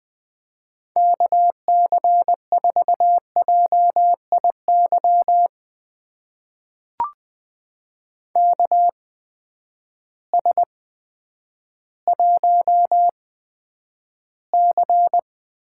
Morse code, heard as a keyed tone, about 20 wpm, spaced as for 10 wpm, 700 Hz. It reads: KC4JIY KS1C
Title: USA Call Signs - 8x Character Spacing - 20wpm